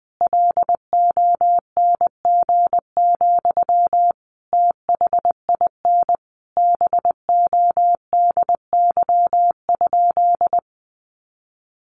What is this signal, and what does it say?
Morse code, heard as a keyed tone, about 20 wpm, 700 Hz. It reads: LONG, THIN BODY?
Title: Sentences from Top 1200 Words 20wpm